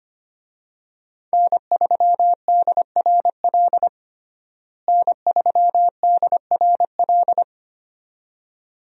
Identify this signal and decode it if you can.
Morse code, heard as a keyed tone, about 25 wpm, 700 Hz. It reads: N3DRL N3DRL
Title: USA Call Signs - 5 Characters - Repeated 2x - 25wpm